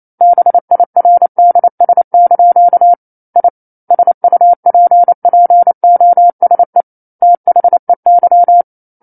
Morse code, heard as a keyed tone, about 28 wpm, 700 Hz. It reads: BIRDS) I SUPPOSE THEY
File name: 09-Chapter-09 28 WPM